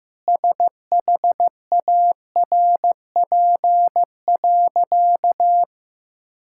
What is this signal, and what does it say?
Morse code, heard as a keyed tone, about 16 wpm, 700 Hz. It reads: SHARP.
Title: Sentences from Top 1600 Words 15wpm